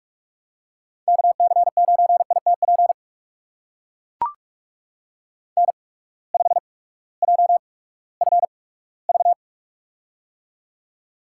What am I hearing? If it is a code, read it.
Morse code, heard as a keyed tone, about 44 wpm, spaced as for 20 wpm, 700 Hz. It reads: KC9ITP N5JFV
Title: USA Call Signs - 8x Character Spacing - 45wpm